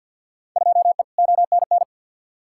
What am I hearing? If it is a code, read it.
Morse code, heard as a keyed tone, about 52 wpm, 700 Hz. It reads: 2E ONN
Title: Field Day Exchanges Encore - 50wpm